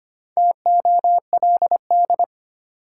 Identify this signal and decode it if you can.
Morse code, heard as a keyed tone, about 25 wpm, 700 Hz. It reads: TOLD
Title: Sentences from Top 400 Words - New Word Drill - 25wpm